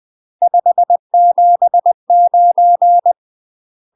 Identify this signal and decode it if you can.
Morse code, heard as a keyed tone, about 20 wpm, 700 Hz. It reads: 579